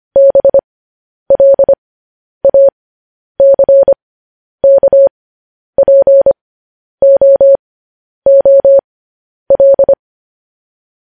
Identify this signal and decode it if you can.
Morse code, heard as a keyed tone, about 25 wpm, spaced as for 10 wpm, 550 Hz. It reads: BLACKPOOL
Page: news-headlines-in-morse-code-at-10-wpm